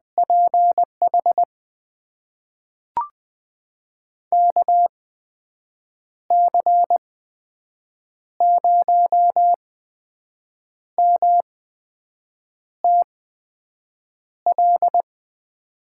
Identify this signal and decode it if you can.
Morse code, heard as a keyed tone, about 20 wpm, spaced as for 6 wpm, 700 Hz. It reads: PH KC0MTL
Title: USA Call Signs - 6 Characters - 8x Character Spacing - 20wpm